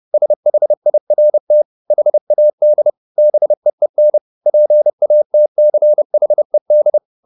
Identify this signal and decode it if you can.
Morse code, heard as a keyed tone, about 30 wpm, 600 Hz. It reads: SHIRT HAD BEEN PATCHED
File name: the old man and the sea / the old man and the sea 1 30wpm